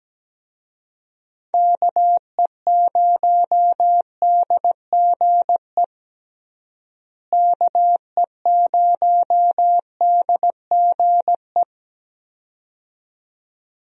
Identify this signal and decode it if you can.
Morse code, heard as a keyed tone, about 17 wpm, 700 Hz. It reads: KE0DGE KE0DGE
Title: USA Call Signs - 6 Characters - Repeated 2x - 17wpm